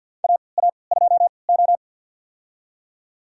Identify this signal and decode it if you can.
Morse code, heard as a keyed tone, about 48 wpm, spaced as for 28 wpm, 700 Hz. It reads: AA2O
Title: USA Call Signs - 3x Character Spacing - 50wpm